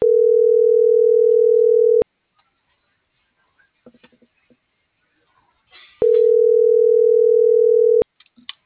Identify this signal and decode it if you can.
Ringback tone (North America), 440 + 480 Hz.